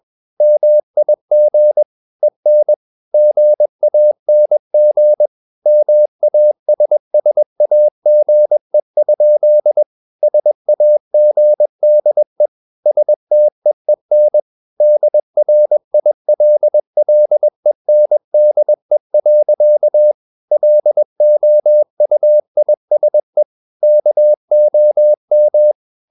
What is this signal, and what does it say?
Morse code, heard as a keyed tone, about 21 wpm, 600 Hz. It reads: MIG EN GANG MASSAGE? SAGDE STEEN DRILLENDE. LOUISE KOM
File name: Ferie QRS 21wpm